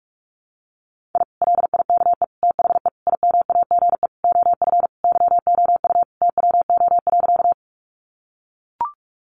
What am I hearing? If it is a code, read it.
Morse code, heard as a keyed tone, about 44 wpm, 700 Hz. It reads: I LIKE THE IMAGE OF YOU TWO.
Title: Sentences from Top 700 Words 45wpm